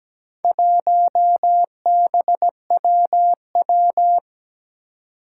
Morse code, heard as a keyed tone, about 17 wpm, 700 Hz. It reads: 1BWW
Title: Four Challenging Letter-Number Combinations 17wpm